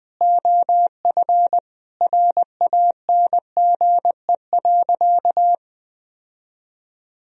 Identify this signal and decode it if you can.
Morse code, heard as a keyed tone, about 21 wpm, 700 Hz. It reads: OF RANGE.
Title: Sentences from Top 1000 Words - Review Part 4 - 20wpm